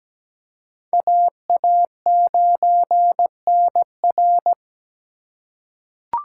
Morse code, heard as a keyed tone, about 17 wpm, 700 Hz. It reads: AA9NR
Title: USA Call Signs - 3x Character Spacing - 17wpm